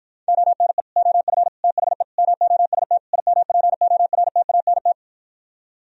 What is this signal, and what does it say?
Morse code, heard as a keyed tone, about 56 wpm, 700 Hz. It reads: ONE OF THE MOST IMPORTANT